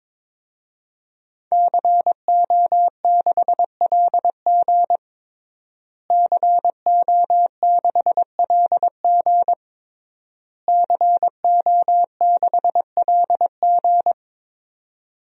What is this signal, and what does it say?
Morse code, heard as a keyed tone, about 22 wpm, 700 Hz. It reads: CO6LG CO6LG CO6LG
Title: DXCC-100 Call Signs - 5 Characters - Repeated 3x - 22wpm